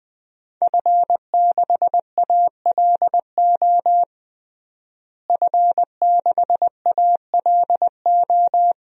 Morse code, heard as a keyed tone, about 20 wpm, 700 Hz. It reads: F6ALO F6ALO
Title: DXCC-100 Call Signs - 5 Characters - Repeated 3x - 20wpm